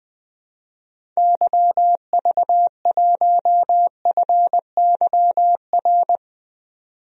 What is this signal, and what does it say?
Morse code, heard as a keyed tone, about 20 wpm, 700 Hz. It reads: YV1FYR
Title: DXCC-100 Call Signs - 6 Characters - 20wpm